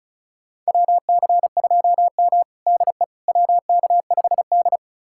Morse code, heard as a keyed tone, about 36 wpm, 700 Hz. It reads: WC2M DE WK5D
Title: USA Callsign DE Callsign - 4 Characters - 35wpm